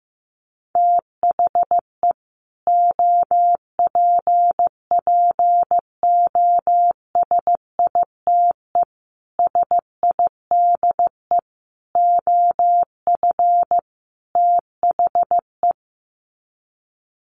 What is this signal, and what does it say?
Morse code, heard as a keyed tone, about 15 wpm, 700 Hz. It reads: THE OPPOSITE SIDE OF THE
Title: Sets of 5 Words Encore 15wpm